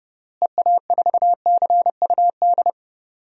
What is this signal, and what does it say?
Morse code, heard as a keyed tone, about 32 wpm, 700 Hz. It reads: EA4CUD
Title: DXCC-100 Call Signs - 6 Characters - Repeated 2x - 30wpm